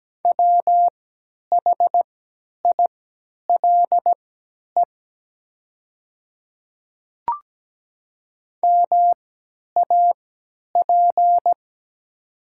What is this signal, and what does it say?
Morse code, heard as a keyed tone, about 17 wpm, spaced as for 10 wpm, 700 Hz. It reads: WHILE MAP
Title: 3x Character Spacing - Top 500 Words - 17wpm